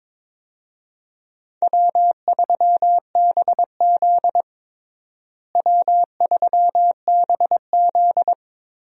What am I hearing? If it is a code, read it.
Morse code, heard as a keyed tone, about 22 wpm, 700 Hz. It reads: W3BZ W3BZ
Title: USA Call Signs - 4 Characters - Repeated 3x - 22wpm